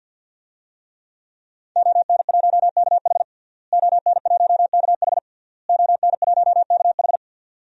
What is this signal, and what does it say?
Morse code, heard as a keyed tone, about 46 wpm, 700 Hz. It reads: ON1KH ON1KH ON1KH
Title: DXCC-100 Call Signs - 5 Characters - Repeated 3x - 50wpm